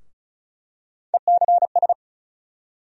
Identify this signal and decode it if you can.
Morse code, heard as a keyed tone, about 36 wpm, 700 Hz. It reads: ECS
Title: USA Call Sign Postfixes - 3 Letters 35wpm